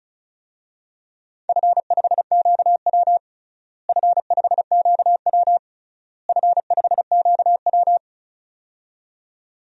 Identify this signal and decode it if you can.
Morse code, heard as a keyed tone, about 36 wpm, 700 Hz. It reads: F5QW F5QW F5QW
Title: DXCC-100 Call Signs - 4 Characters - Repeated 3x - 35wpm